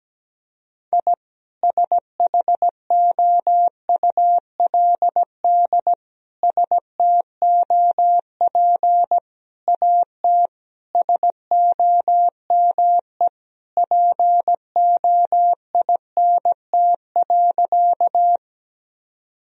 Morse code, heard as a keyed tone, about 17 wpm, 700 Hz. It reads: I SHOULD STOP AT SOME POINT.
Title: Sentences from Top 1000 Words - Review Part 1 - 17wpm